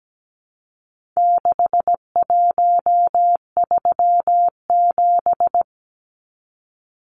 Morse code, heard as a keyed tone, about 17 wpm, 700 Hz. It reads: 6137
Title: Numbers - 4-Digit 17wpm